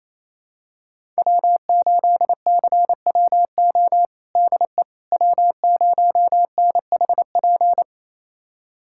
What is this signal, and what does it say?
Morse code, heard as a keyed tone, about 28 wpm, 700 Hz. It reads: W8CWO DE W0NHP
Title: USA Callsign DE Callsign - 5 Characters - 28wpm